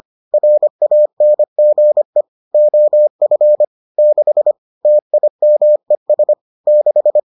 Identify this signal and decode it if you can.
Morse code, heard as a keyed tone, about 25 wpm, 600 Hz. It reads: RANGE OF 6 TIMES 6